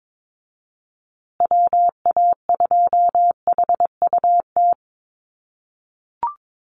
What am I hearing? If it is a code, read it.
Morse code, heard as a keyed tone, about 22 wpm, 700 Hz. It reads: WA2HUT
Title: USA Call Signs Repeated 2x 22wpm